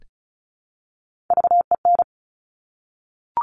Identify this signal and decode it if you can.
Morse code, heard as a keyed tone, about 36 wpm, 700 Hz. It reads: VEN
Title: Top Three Letter Combinations in Words 35wpm